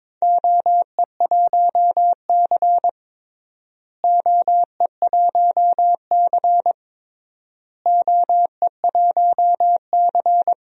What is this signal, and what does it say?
Morse code, heard as a keyed tone, about 22 wpm, 700 Hz. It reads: OE1C OE1C OE1C
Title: DXCC-100 Call Signs - 4 Characters - Repeated 3x - 22wpm